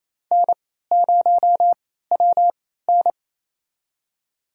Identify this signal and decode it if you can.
Morse code, heard as a keyed tone, about 28 wpm, spaced as for 16 wpm, 700 Hz. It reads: N0WN
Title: USA Call Signs - 4 Characters - 3x Character Spacing - 28wpm